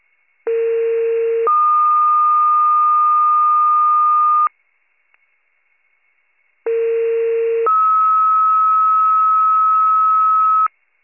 Two-tone page, heard as a one-second tone, then a three-second tone, about 460 then 1220 Hz, about 460 then 1290 Hz. It is heard from 0.5 to 10.7 s.